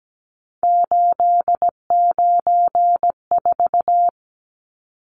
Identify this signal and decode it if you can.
Morse code, heard as a keyed tone, about 17 wpm, 700 Hz. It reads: 894
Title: Numbers - 3-Digit 17wpm